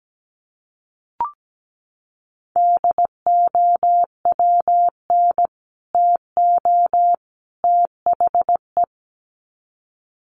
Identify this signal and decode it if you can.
Morse code, heard as a keyed tone, about 17 wpm, 700 Hz. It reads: DOWN TO THE
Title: Sets of 3 Words 17wpm